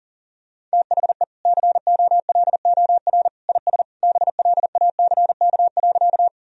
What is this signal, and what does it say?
Morse code, heard as a keyed tone, about 40 wpm, 700 Hz. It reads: THE COLOR IS BLACK.
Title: Sentences from Top 1000 Words - Review Part 2 - 40wpm